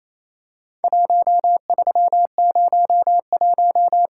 Morse code, heard as a keyed tone, about 28 wpm, 700 Hz. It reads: 1301
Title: Worked All Europe DX Contest - QTC Exchange - 28wpm